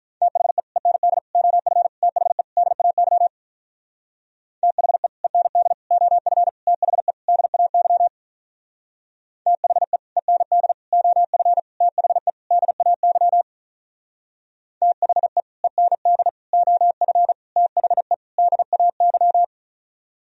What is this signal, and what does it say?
Morse code, heard as a keyed tone, about 44 wpm, 700 Hz. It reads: THE END OF THE DAY THE END OF THE DAY THE END OF THE DAY THE END OF THE DAY